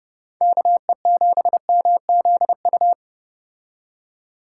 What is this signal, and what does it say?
Morse code, heard as a keyed tone, about 30 wpm, 700 Hz. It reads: KE7MZU